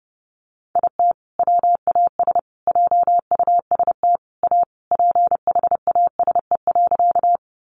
Morse code, heard as a keyed tone, about 30 wpm, 700 Hz. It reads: IT WAS JUST A PHASE.
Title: Sentences from Top 1200 Words 30wpm